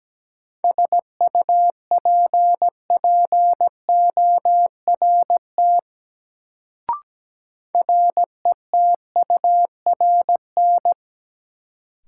Morse code, heard as a keyed tone, about 17 wpm, 700 Hz. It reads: SUPPORT RETURN
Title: Sentences from Top 800 Words - New Word Drill - 17wpm